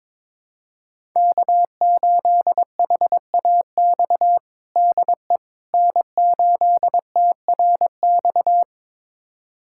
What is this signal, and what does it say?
Morse code, heard as a keyed tone, about 22 wpm, 700 Hz. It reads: K8HAX DE N8TRX
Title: USA Callsign DE Callsign - 5 Characters - 22wpm